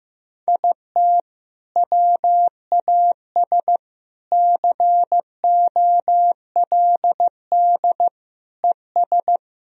Morse code, heard as a keyed tone, about 15 wpm, 700 Hz. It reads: IT WAS COLD ES
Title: Sentences from Top 500 Words and Abbreviations 15wpm